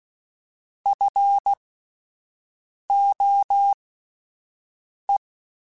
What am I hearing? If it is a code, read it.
Morse code, heard as a keyed tone, about 16 wpm, spaced as for 5 wpm, 800 Hz. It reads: FOE
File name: Alphabetical / Alphabetical mp3s / Fs @5